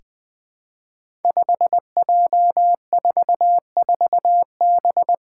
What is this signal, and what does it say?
Morse code, heard as a keyed tone, about 20 wpm, 700 Hz. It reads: 5J44B
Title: Five Challenging Letter-Number Combinations 20wpm